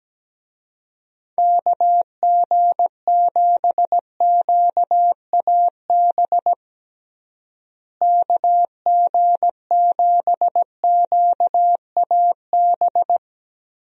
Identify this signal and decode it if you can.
Morse code, heard as a keyed tone, about 17 wpm, 700 Hz. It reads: KG7QAB KG7QAB
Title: USA Call Signs - 6 Characters - Repeated 3x - 17wpm